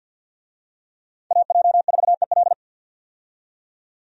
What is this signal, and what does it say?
Morse code, heard as a keyed tone, about 49 wpm, 700 Hz. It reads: AJ4EL
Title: USA Call Signs - 5 Characters - 50wpm